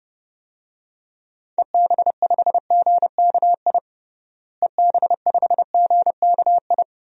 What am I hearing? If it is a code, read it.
Morse code, heard as a keyed tone, about 32 wpm, 700 Hz. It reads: EB5GKI EB5GKI
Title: DXCC-100 Call Signs - 6 Characters - Repeated 2x - 30wpm